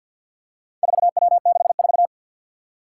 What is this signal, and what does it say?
Morse code, heard as a keyed tone, about 54 wpm, 700 Hz. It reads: 4W64
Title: Four Challenging Letter-Number Combinations 50wpm